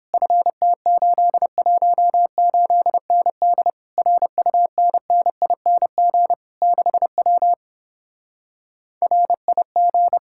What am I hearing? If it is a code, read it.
Morse code, heard as a keyed tone, about 29 wpm, 700 Hz. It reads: FT818ND RUNNING 6W RIG